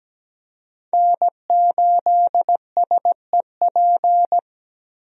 Morse code, heard as a keyed tone, about 17 wpm, 700 Hz. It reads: N8SEP
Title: USA Call Signs - 5 Characters - Repeated 3x - 17wpm